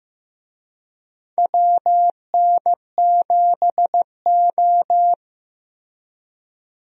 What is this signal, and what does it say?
Morse code, heard as a keyed tone, about 15 wpm, 700 Hz. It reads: WN7O
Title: USA Call Signs - 4 Characters - Repeated 3x - 15wpm